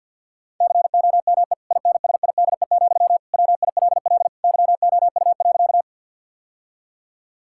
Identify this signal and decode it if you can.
Morse code, heard as a keyed tone, about 50 wpm, 700 Hz. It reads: COME INSIDE, WILL YOU.
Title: Sentences from Top 1000 Words - Review Part 6 - 50wpm